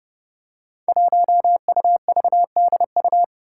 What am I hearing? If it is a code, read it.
Morse code, heard as a keyed tone, about 30 wpm, 700 Hz. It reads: 1UVDU
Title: Five Challenging Letter-Number Combinations 30wpm